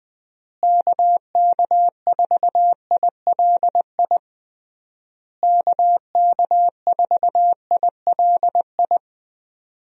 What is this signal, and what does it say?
Morse code, heard as a keyed tone, about 20 wpm, 700 Hz. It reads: KK4ILI KK4ILI
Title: USA Call Signs - 6 Characters - Repeated 3x - 20wpm